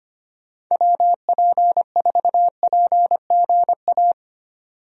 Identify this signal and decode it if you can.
Morse code, heard as a keyed tone, about 25 wpm, 700 Hz. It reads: WP4PGA
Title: USA Call Signs - 6 Characters - 25wpm